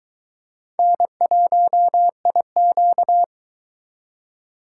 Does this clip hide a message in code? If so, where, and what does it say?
Yes: Morse code from 0.8 to 3.2 s, heard as a keyed tone, about 22 wpm, 700 Hz. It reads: N1IQ